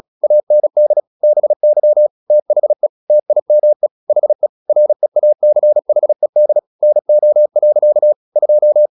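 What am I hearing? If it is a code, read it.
Morse code, heard as a keyed tone, about 36 wpm, 600 Hz. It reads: AND BY THE TIME HE REACHED NO. 2
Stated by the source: Morse practice at 36 wpm, text of Ken Follett's novel Never, from CW med Gnister.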